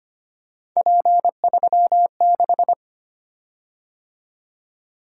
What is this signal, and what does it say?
Morse code, heard as a keyed tone, about 25 wpm, 700 Hz. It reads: P36
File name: Three Letter-Number Combinations - 25wpm